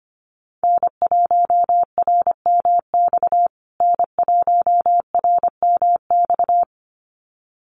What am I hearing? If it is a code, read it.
Morse code, heard as a keyed tone, about 25 wpm, 700 Hz. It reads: N1RMX N1RMX
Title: USA Call Signs Repeated 2x 25wpm